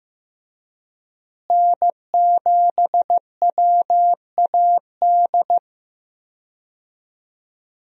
Morse code, heard as a keyed tone, about 15 wpm, 700 Hz. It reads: N7WAD
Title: USA Call Signs - 5 Characters - Repeated 2x - 15wpm